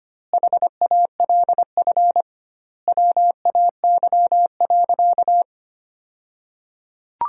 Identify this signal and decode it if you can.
Morse code, heard as a keyed tone, about 25 wpm, 700 Hz. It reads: HALF WAY.
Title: 2x Word-Spacing - Sentences from Top 400 Words - 25wpm